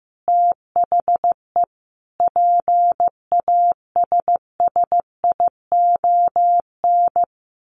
Morse code, heard as a keyed tone, about 15 wpm, 700 Hz. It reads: THE PASSION
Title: Sentences from Top 1400 Words 15wpm